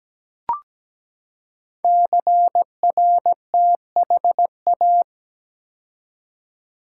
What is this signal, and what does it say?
Morse code, heard as a keyed tone, about 17 wpm, 700 Hz. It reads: CRTHA
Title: Code Groups - 5 Letters - 17wpm